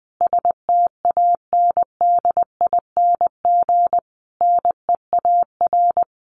Morse code, heard as a keyed tone, about 20 wpm, 700 Hz. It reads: STANDING NEAR
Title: A Sound Of Thunder 20wpm